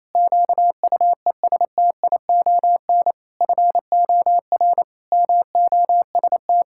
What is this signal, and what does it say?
Morse code, heard as a keyed tone, about 28 wpm, 700 Hz. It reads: QUESTION FOR MOST
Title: Pangram Challenge v2 - 28wpm